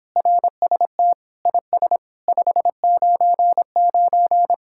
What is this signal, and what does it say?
Morse code, heard as a keyed tone, about 26 wpm, 700 Hz. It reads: RST IS 599